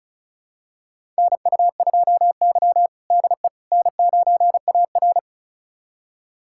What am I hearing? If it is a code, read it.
Morse code, heard as a keyed tone, about 36 wpm, 700 Hz. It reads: NU2Y DE N9AR
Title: USA Callsign DE Callsign - 4 Characters - 35wpm